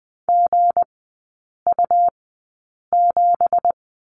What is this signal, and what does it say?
Morse code, heard as a keyed tone, about 20 wpm, spaced as for 8 wpm, 700 Hz. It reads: GU7
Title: 166 Lesson 27e - Number 7 - Three Characters 20wpm